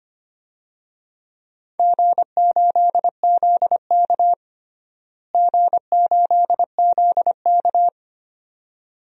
Morse code, heard as a keyed tone, about 25 wpm, 700 Hz. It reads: G8ZK G8ZK
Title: DXCC-100 Call Signs - 4 Characters - Repeated 2x - 25wpm